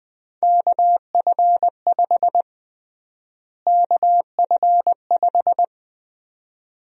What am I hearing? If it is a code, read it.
Morse code, heard as a keyed tone, about 20 wpm, 700 Hz. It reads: KF5 KF5
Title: USA Call Sign Prefixes - Repeated 2x - 20wpm